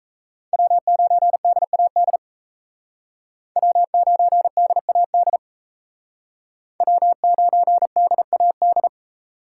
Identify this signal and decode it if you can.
Morse code, heard as a keyed tone, about 36 wpm, 700 Hz. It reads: W9DAD W9DAD W9DAD